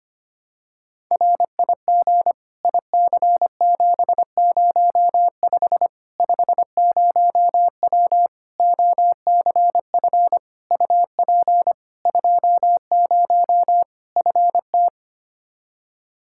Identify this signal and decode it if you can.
Morse code, heard as a keyed tone, about 25 wpm, 700 Hz. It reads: RIG IC705 50W OCF UP 20 FT
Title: Practice QSO - Exchange 2 - Info - 25wpm